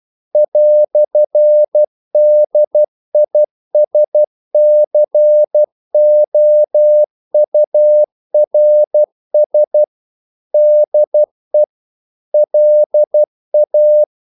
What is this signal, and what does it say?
Morse code, heard as a keyed tone, about 12 wpm, 600 Hz. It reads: "DISCOURS DE LA
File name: CW / Practice / EnglishQuotations0112